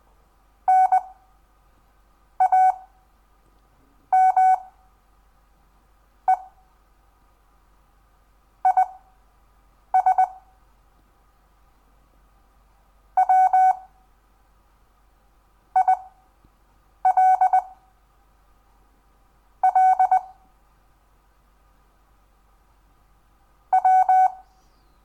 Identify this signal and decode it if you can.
Morse code, heard as a keyed tone, about 19 wpm, spaced as for 5 wpm, 750 Hz. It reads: NAME IS WILL W
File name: cw / Lesson10 REVIEW